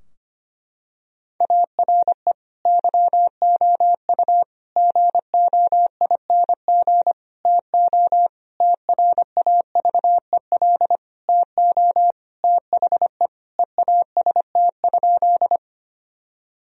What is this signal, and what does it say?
Morse code, heard as a keyed tone, about 25 wpm, 700 Hz. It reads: ARE YOU GOING TO TRAVEL TO THE EAST?
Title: Sentences from Top 1000 Words - Review Part 3 - 25wpm